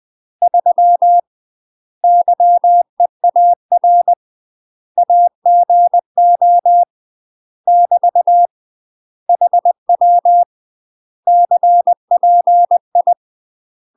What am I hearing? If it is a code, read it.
Morse code, heard as a keyed tone, about 20 wpm, 700 Hz. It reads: 3 YEAR AGO <BT> HW CPI
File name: UK MASTER QSOs TRAINING / 20WPM / UK MASTER QSOs 0026